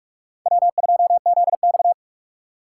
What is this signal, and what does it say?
Morse code, heard as a keyed tone, about 46 wpm, 700 Hz. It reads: W2ZX
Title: USA Call Signs - 2x Character Spacing - 45wpm